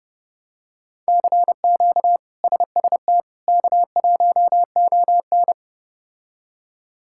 Morse code, heard as a keyed tone, about 30 wpm, 700 Hz. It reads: CQ SST K1ON
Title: K1USN SST QSO Practice - 30wpm